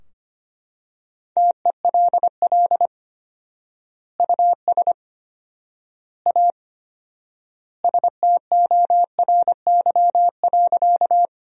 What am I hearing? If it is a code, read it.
Morse code, heard as a keyed tone, about 25 wpm, 700 Hz. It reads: TELL US A STORY.
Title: 4x Word-Spacing - Sentences from Top 300 Words - 25wpm